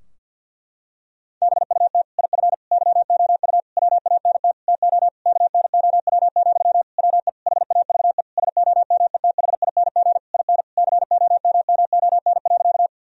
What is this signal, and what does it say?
Morse code, heard as a keyed tone, about 50 wpm, 700 Hz. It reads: BUT IF YOU WANT TO KNOW, WE HAVE SOMETHING IN COMMON.
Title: Sentences from Top 1000 Words - Review Part 3 - 50wpm